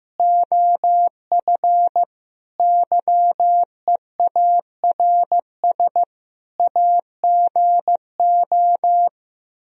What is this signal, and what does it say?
Morse code, heard as a keyed tone, about 15 wpm, 700 Hz. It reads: OF YEARS AGO
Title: Speed-Racing - Sets of 5 Words 15wpm